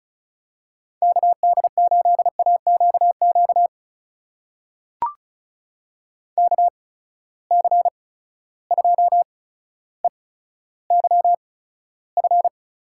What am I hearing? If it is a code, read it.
Morse code, heard as a keyed tone, about 36 wpm, spaced as for 15 wpm, 700 Hz. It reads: KD8AQQ KC2EYF
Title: USA Call Signs - 6 Characters - 8x Character Spacing - 35wpm